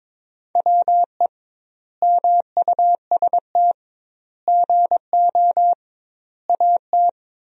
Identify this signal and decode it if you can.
Morse code, heard as a keyed tone, about 22 wpm, 700 Hz. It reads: WE MUST GO AT
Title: 2x Word-Spacing - Sentences from Top 300 Words - 22wpm